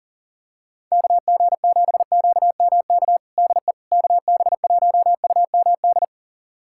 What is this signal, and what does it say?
Morse code, heard as a keyed tone, about 40 wpm, 700 Hz. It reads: KG7QMK DE KB1UMD
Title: USA Callsign DE Callsign - 6 Characters - 40wpm